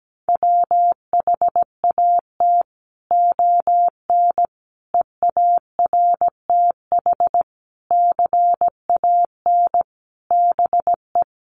Morse code, heard as a keyed tone, about 17 wpm, 700 Hz. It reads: WHAT ON EARTH CAN BE
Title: Sentences from Top 700 Words 17wpm